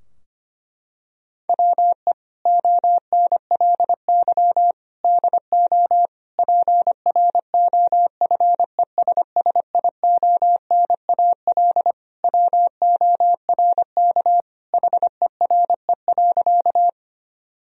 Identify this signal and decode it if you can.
Morse code, heard as a keyed tone, about 25 wpm, 700 Hz. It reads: WE ONLY DO PROFESSIONAL WORK HERE.
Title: Sentences from Top 1000 Words - Review Part 4 - 25wpm